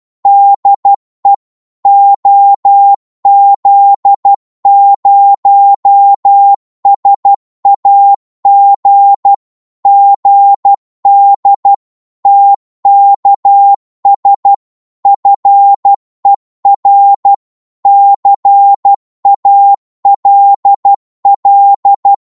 Morse code, heard as a keyed tone, about 12 wpm, 800 Hz. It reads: DE OZ0SAG GD TKS FER CALL
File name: Morse / qso-12wpm